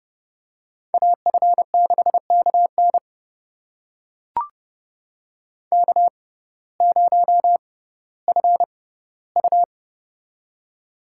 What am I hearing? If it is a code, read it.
Morse code, heard as a keyed tone, about 30 wpm, spaced as for 17 wpm, 700 Hz. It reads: AF6KN K0FU
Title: USA Call Signs - 6x Character Spacing - 30wpm